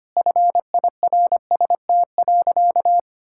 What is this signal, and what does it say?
Morse code, heard as a keyed tone, about 25 wpm, 700 Hz. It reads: FIRST.
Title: Sentences from Top 1000 Words - Review Part 6 - 25wpm